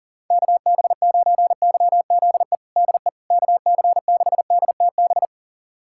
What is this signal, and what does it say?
Morse code, heard as a keyed tone, about 37 wpm, 700 Hz. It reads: KB9YZE DE KC6DTB
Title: USA Callsign DE Callsign - 6 Characters - 40wpm